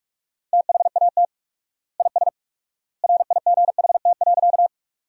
Morse code, heard as a keyed tone, about 42 wpm, 700 Hz. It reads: THAT IS RIGHT.
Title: 4x Word-Spacing - Sentences from Top 200 Words - 45wpm